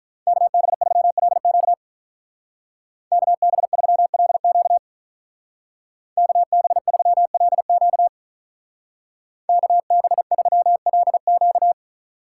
Morse code, heard as a keyed tone, about 43 wpm, 700 Hz. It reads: KB3LQ KB3LQ KB3LQ KB3LQ
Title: Speed Racing - USA 2x2 Call Signs 35wpm